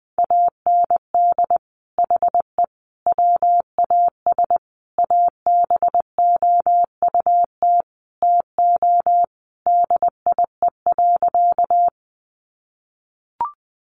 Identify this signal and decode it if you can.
Morse code, heard as a keyed tone, about 20 wpm, 700 Hz. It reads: AND HE WAS ABOUT TO DIE.
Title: Sentences from Top 1300 Words 20wpm